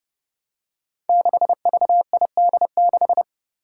Morse code, heard as a keyed tone, about 30 wpm, 700 Hz. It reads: 6VID6
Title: Five Challenging Letter-Number Combinations 30wpm